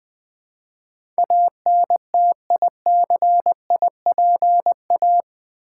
Morse code, heard as a keyed tone, about 20 wpm, 700 Hz. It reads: ANTICIPA
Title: Build A Word - 4 Letters At A Time - 10-Letter Words - 20wpm